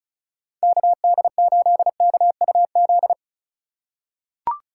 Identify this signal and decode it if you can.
Morse code, heard as a keyed tone, about 36 wpm, 700 Hz. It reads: KD8KUZ
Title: USA Call Signs - 6 Characters - 3x Character Spacing - 35wpm